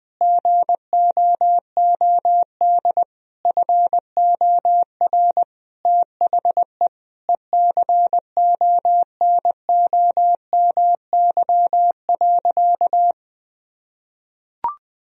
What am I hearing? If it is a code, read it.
Morse code, heard as a keyed tone, about 20 wpm, 700 Hz. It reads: GOOD FOR THE ECONOMY.
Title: Sentences from Top 1000 Words - Review Part 4 - 20wpm